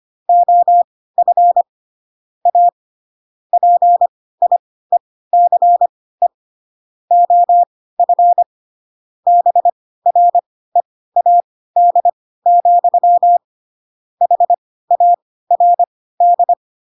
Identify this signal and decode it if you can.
Morse code, heard as a keyed tone, about 25 wpm, spaced as for 16 wpm, 700 Hz. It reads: OF A PIECE OF BREAD, HARD